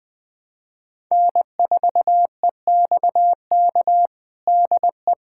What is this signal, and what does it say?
Morse code, heard as a keyed tone, about 20 wpm, 700 Hz. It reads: N4EXK DE
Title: USA Callsign DE Callsign - 5 Characters - 20wpm